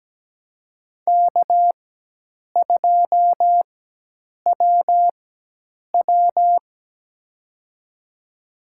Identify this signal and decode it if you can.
Morse code, heard as a keyed tone, about 17 wpm, spaced as for 8 wpm, 700 Hz. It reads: K2WW
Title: USA Call Signs - 4x Character Spacing - 17wpm